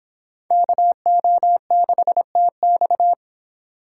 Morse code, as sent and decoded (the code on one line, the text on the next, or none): -.- --- -.... - -..-
KO6TX